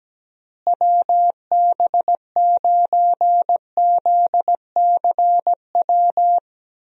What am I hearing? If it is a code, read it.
Morse code, heard as a keyed tone, about 17 wpm, 700 Hz. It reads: WB9ZCW
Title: USA Call Signs - 6 Characters - Repeated 2x - 17wpm